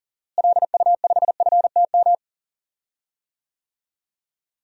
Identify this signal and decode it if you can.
Morse code, heard as a keyed tone, about 37 wpm, 700 Hz. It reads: LU5FTM
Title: DXCC-100 Call Signs - 6 Characters - Repeated 2x - 40wpm